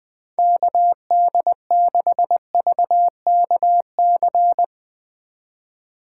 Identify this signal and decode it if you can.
Morse code, heard as a keyed tone, about 20 wpm, 700 Hz. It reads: KD6VKC